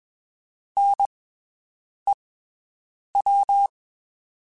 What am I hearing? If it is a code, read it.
Morse code, heard as a keyed tone, about 21 wpm, spaced as for 7 wpm, 800 Hz. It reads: NEW
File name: (spaced) / QSOs Spaced-mp3s / WD0FBT 1 @20 sp